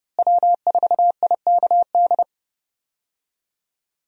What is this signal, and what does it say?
Morse code, heard as a keyed tone, about 30 wpm, 700 Hz. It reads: W4IKD
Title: USA Call Signs - 5 Characters - Repeated 2x - 30wpm